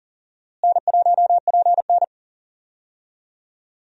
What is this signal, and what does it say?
Morse code, heard as a keyed tone, about 40 wpm, 700 Hz. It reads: N1PN
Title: USA Call Signs - 4 Characters - Repeated 2x - 40wpm